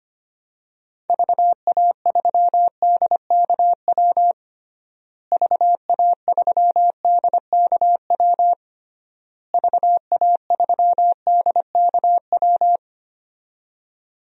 Morse code, heard as a keyed tone, about 25 wpm, 700 Hz. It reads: VA3DKW VA3DKW VA3DKW
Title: DXCC-100 Call Signs - 6 Characters - Repeated 3x - 25wpm